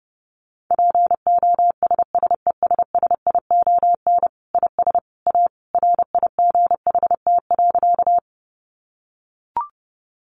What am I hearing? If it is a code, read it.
Morse code, heard as a keyed tone, about 30 wpm, 700 Hz. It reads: POSSESSION IS A RIGHT.
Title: Sentences from Top 1400 Words 30wpm